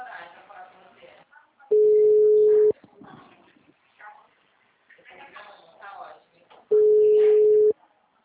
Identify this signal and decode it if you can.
Ringback tone (Europe, 425 Hz), with speech.